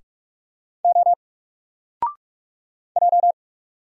Morse code, heard as a keyed tone, about 44 wpm, 700 Hz. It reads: O J
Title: Single Letter-Number Alt 45wpm